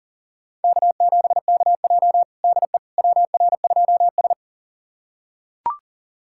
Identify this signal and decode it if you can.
Morse code, heard as a keyed tone, about 40 wpm, 700 Hz. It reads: K7KJ DE WR2S
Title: USA Callsign DE Callsign - 4 Characters - 40wpm